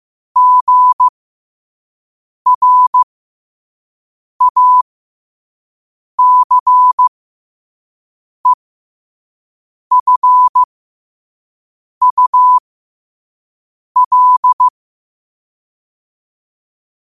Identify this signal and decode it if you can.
Morse code, heard as a keyed tone, about 15 wpm, spaced as for 5 wpm, 1000 Hz. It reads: GRACEFUL